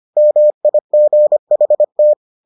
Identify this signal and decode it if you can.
Morse code, heard as a keyed tone, about 25 wpm, 600 Hz. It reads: MIGHT